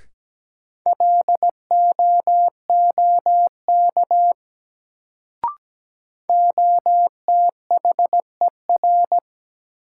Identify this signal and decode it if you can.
Morse code, heard as a keyed tone, about 17 wpm, 700 Hz. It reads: LOOK OTHER
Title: Sentences from Top 100 Words - New Word Drill - 17wpm